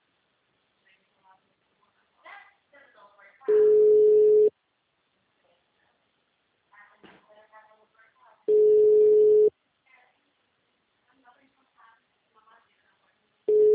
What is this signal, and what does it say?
Ringback tone (Europe, 425 Hz), with speech.